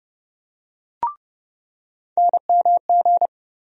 Morse code, heard as a keyed tone, about 31 wpm, 700 Hz. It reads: NMG
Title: USA Call Sign Postfixes - 3 Letters 30wpm